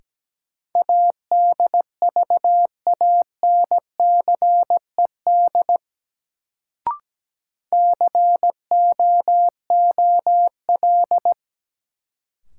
Morse code, heard as a keyed tone, about 17 wpm, 700 Hz. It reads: ADVANCED COOL
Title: Sentences from Top 1200 Words - New Word Drill - 17wpm